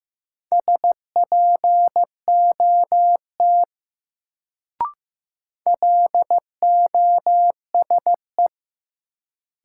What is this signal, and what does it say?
Morse code, heard as a keyed tone, about 16 wpm, 700 Hz. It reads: SPOT LOSE
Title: Sentences from Top 1100 Words - New Word Drill - 15wpm